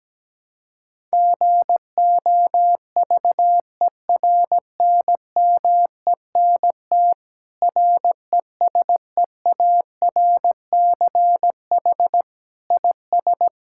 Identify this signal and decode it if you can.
Morse code, heard as a keyed tone, about 17 wpm, 700 Hz. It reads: GOVERNMENT RESEARCH IS
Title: Sentences from Top 1000 Words - Review Part 6 - 17wpm